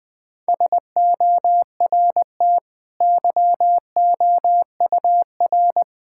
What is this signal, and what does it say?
Morse code, heard as a keyed tone, about 20 wpm, 700 Hz. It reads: SORT YOUR
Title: Easy Sentences with Hints - 20wpm